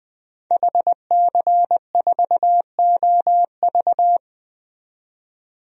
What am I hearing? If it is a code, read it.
Morse code, heard as a keyed tone, about 20 wpm, 700 Hz. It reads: HC4OV
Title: DXCC-100 Call Signs - 5 Characters - 20wpm